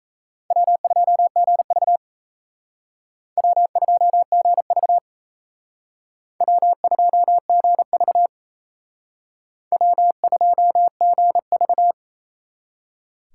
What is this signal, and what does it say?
Morse code, heard as a keyed tone, about 37 wpm, 700 Hz. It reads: W2GV W2GV W2GV W2GV